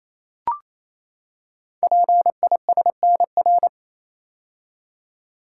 Morse code, heard as a keyed tone, about 27 wpm, 700 Hz. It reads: PISNR
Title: Code Groups - 5 Letters - 28wpm